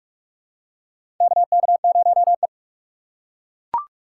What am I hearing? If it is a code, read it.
Morse code, heard as a keyed tone, about 42 wpm, 700 Hz. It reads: KK0E
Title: USA Call Signs - 4 Characters - 4x Character Spacing - 45wpm